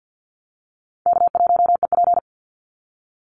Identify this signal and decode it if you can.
Morse code, heard as a keyed tone, about 50 wpm, 700 Hz. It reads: K1EP
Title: Easy Call Signs Repeated 3x - 50wpm